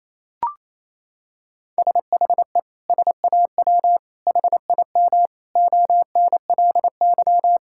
Morse code, heard as a keyed tone, about 28 wpm, 700 Hz. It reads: SHE SAW HIM ONLY
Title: Sentences from Top 1600 Words 28wpm